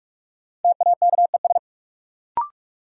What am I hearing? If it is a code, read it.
Morse code, heard as a keyed tone, about 46 wpm, 700 Hz. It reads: TAKES
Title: Sets of 5 Words Encore - New Word Drill - 45wpm